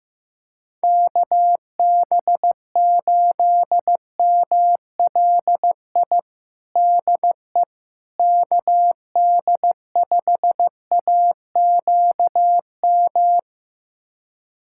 Morse code, heard as a keyed tone, about 15 wpm, 700 Hz. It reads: KB8MLI DE KD5AQM
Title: USA Callsign DE Callsign 15wpm